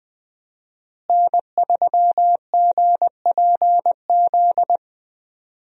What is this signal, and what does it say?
Morse code, heard as a keyed tone, about 20 wpm, 700 Hz. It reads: N3GPZ
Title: USA Call Signs - 5 Characters - Repeated 2x - 20wpm